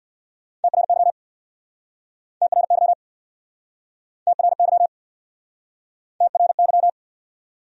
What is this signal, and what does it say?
Morse code, heard as a keyed tone, about 67 wpm, 700 Hz. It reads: TRY TRY TRY TRY